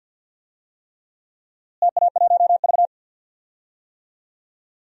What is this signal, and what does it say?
Morse code, heard as a keyed tone, about 50 wpm, 700 Hz. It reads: TA1V
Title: DXCC-100 Call Signs - 4 Characters - Repeated 3x - 50wpm